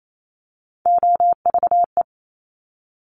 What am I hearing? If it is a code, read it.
Morse code, heard as a keyed tone, about 28 wpm, 700 Hz. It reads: OVE
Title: Top Three Letter Combinations in Words 28wpm